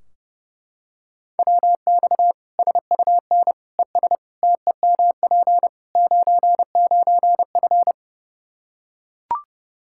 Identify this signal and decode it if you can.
Morse code, heard as a keyed tone, about 32 wpm, 700 Hz. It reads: WX SUN ES TEMP 99F
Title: Practice QSO - Exchange 2 - Info - 30wpm